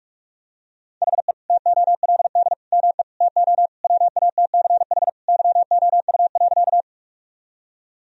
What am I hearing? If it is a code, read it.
Morse code, heard as a keyed tone, about 43 wpm, 700 Hz. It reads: HE TOLD ME TO WATCH YOU.
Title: Sentences from Top 1000 Words - Review Part 1 - 45wpm